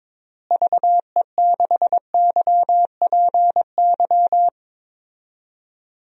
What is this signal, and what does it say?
Morse code, heard as a keyed tone, about 22 wpm, 700 Hz. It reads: VE6YPY
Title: DXCC-100 Call Signs - 6 Characters - Repeated 2x - 22wpm